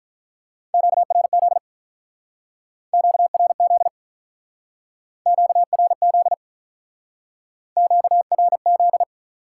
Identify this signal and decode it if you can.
Morse code, heard as a keyed tone, about 47 wpm, 700 Hz. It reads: QRZ QRZ QRZ QRZ